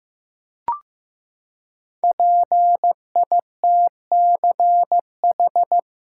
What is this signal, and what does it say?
Morse code, heard as a keyed tone, about 15 wpm, 700 Hz. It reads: PITCH
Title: Five Letter Words - 15wpm